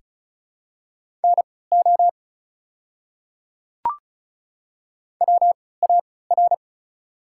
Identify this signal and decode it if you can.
Morse code, heard as a keyed tone, about 36 wpm, spaced as for 20 wpm, 700 Hz. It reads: NO WAR
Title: 3x Character Spacing - Top 500 Words - 35wpm